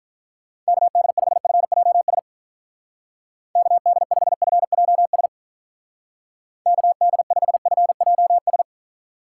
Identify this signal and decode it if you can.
Morse code, heard as a keyed tone, about 47 wpm, 700 Hz. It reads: KD5FJS KD5FJS KD5FJS